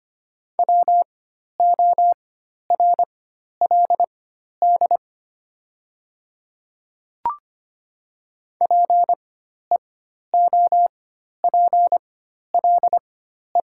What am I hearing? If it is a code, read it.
Morse code, heard as a keyed tone, about 25 wpm, spaced as for 12 wpm, 700 Hz. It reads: WORLD PEOPLE
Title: 4x Character Spacing - Top 200 Words - 25wpm